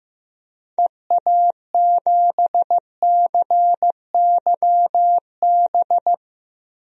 Morse code, heard as a keyed tone, about 16 wpm, 700 Hz. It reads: EA7CYB
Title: DXCC-100 Call Signs - 6 Characters - Repeated 3x - 15wpm